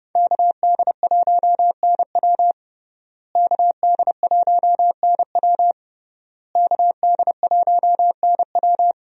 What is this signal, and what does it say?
Morse code, heard as a keyed tone, about 29 wpm, 700 Hz. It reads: KD1NW KD1NW KD1NW
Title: USA Call Signs - 5 Characters - Repeated 3x - 30wpm